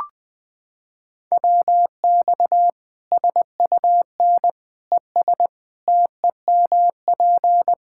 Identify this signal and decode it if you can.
Morse code, heard as a keyed tone, about 20 wpm, 700 Hz. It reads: WX SUN ES TEMP
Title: Practice QSO - Exchange 2 - Info - 20wpm